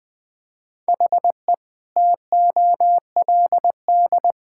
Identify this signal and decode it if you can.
Morse code, heard as a keyed tone, about 20 wpm, 700 Hz. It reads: HE TOLD
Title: Sentences from Top 1600 Words 20wpm